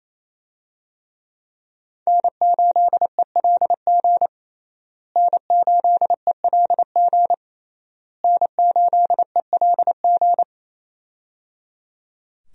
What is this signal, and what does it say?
Morse code, heard as a keyed tone, about 28 wpm, 700 Hz. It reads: N8ELG N8ELG N8ELG
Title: USA Call Signs - 5 Characters - Repeated 3x - 28wpm